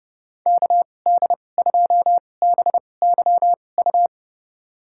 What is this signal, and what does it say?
Morse code, heard as a keyed tone, about 30 wpm, spaced as for 22 wpm, 700 Hz. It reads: KD2BYU
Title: USA Call Signs - 6 Characters - 2x Character Spacing - 30wpm